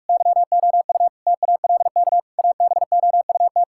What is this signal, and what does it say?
Morse code, heard as a keyed tone, about 46 wpm, 700 Hz. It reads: YOU TALK ABOUT
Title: Sentences from Top 1000 Words - Review Part 6 - 45wpm